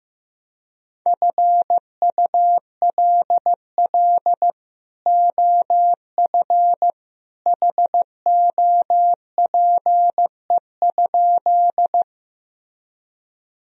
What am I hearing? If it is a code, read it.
Morse code, heard as a keyed tone, about 15 wpm, 700 Hz. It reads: FULL OF HOPE?
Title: Sentences from Top 1000 Words - Review Part 3 - 15wpm